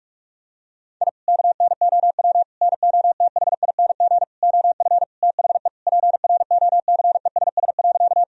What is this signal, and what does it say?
Morse code, heard as a keyed tone, about 46 wpm, 700 Hz. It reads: I KNOW NOTHING OF THE PROCESS.
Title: Sentences from Top 1000 Words - Review Part 3 - 45wpm